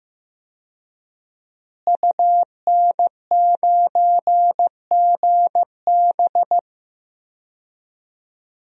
Morse code, heard as a keyed tone, about 16 wpm, 700 Hz. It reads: UN9GB
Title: DXCC-100 Call Signs - 5 Characters - Repeated 2x - 15wpm